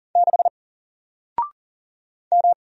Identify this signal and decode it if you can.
Morse code, heard as a keyed tone, about 39 wpm, 700 Hz. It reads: 6 M
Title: Single Character (Letter:Number:Punctuation:Prosign) 40wpm